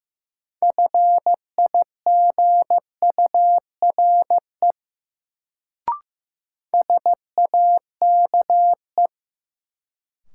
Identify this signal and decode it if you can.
Morse code, heard as a keyed tone, about 15 wpm, 700 Hz. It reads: FIGURE SAKE
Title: Sets of 4 Words - New Word Drill - 15wpm